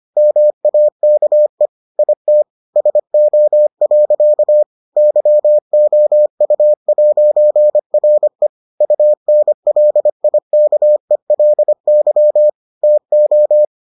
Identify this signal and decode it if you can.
Morse code, heard as a keyed tone, about 25 wpm, 600 Hz. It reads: MAKE IT SO. YOU'RE UNLIKELY TO